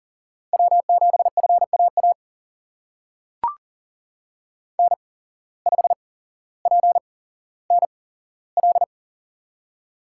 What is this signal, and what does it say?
Morse code, heard as a keyed tone, about 40 wpm, spaced as for 17 wpm, 700 Hz. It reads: W7FAA N5PNL
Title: USA Call Signs - 5 Characters - 8x Character Spacing - 40wpm